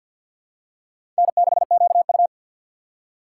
Morse code, heard as a keyed tone, about 52 wpm, 700 Hz. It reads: N6QU
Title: USA Call Signs - 4 Characters - 3x Character Spacing - 50wpm